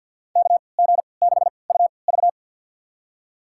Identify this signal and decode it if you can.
Morse code, heard as a keyed tone, about 52 wpm, spaced as for 28 wpm, 690 Hz. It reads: KG6UV
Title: USA Call Signs - 5 Characters - 3x Character Spacing - 50wpm